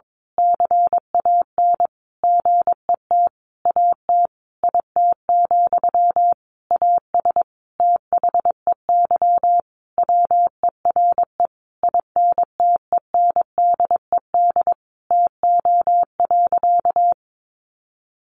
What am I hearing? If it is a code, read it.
Morse code, heard as a keyed tone, about 22 wpm, 700 Hz. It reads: CAN GET AT IT, AS THEY WERE INTENDED TO.
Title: A Sound Of Thunder 22wpm